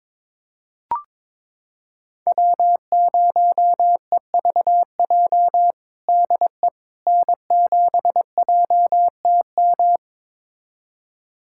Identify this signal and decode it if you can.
Morse code, heard as a keyed tone, about 22 wpm, 700 Hz. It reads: W0EVJ DE N7JTM
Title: USA Callsign DE Callsign - 5 Characters - 22wpm